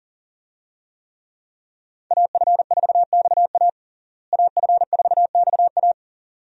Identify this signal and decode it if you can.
Morse code, heard as a keyed tone, about 40 wpm, 700 Hz. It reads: AF4XA AF4XA
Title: USA Call Signs - 5 Characters - Repeated 2x - 40wpm